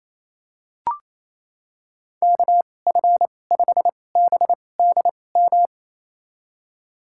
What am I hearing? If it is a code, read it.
Morse code, heard as a keyed tone, about 28 wpm, spaced as for 20 wpm, 700 Hz. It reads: KF5BDM
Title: USA Call Signs - 6 Characters - 2x Character Spacing - 28wpm